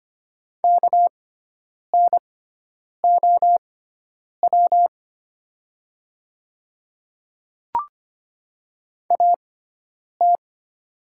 Morse code, heard as a keyed tone, about 25 wpm, spaced as for 9 wpm, 700 Hz. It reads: KNOW AT
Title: 6x Character Spacing - Top 100 Words - 25wpm